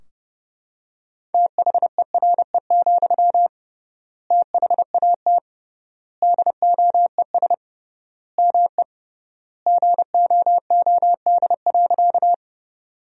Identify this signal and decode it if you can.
Morse code, heard as a keyed tone, about 30 wpm, 700 Hz. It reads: THERE, THAT DOES ME GOOD.
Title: 3x Word-Spacing - Sentences from Top 200 Words - 30wpm